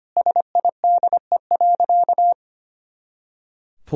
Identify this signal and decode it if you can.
Morse code, heard as a keyed tone, about 25 wpm, 700 Hz. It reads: SIDE.
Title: Sentences from Top 400 Words and Abbreviations 25wpm